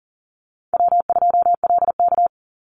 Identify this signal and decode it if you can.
Morse code, heard as a keyed tone, about 41 wpm, 700 Hz. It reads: W2LK
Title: Easy Call Signs - 40wpm